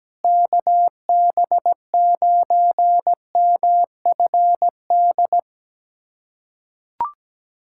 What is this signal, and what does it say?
Morse code, heard as a keyed tone, about 17 wpm, 700 Hz. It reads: KB9MFD
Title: USA Callsign DE Callsign - 6 Characters - 17wpm